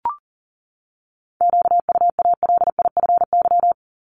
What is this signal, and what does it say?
Morse code, heard as a keyed tone, about 40 wpm, 700 Hz. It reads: QUALIFY
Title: Top 1751-2000 Words 40wpm